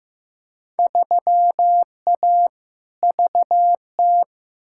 Morse code, heard as a keyed tone, about 15 wpm, 700 Hz. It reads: 3A VT
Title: Field Day Exchanges Encore - 15wpm